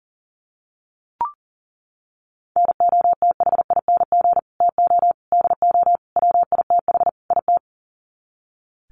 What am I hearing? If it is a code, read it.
Morse code, heard as a keyed tone, about 40 wpm, 700 Hz. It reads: NOTHING TO DO WITH IT
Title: Sets of 5 Words Encore 40wpm